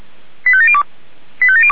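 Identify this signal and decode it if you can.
Five-tone selective call (ZVEI), heard as quick runs of five tones, tones of about 70 ms. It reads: 85702
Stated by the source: ZVEI five-tone selective call received by SDR.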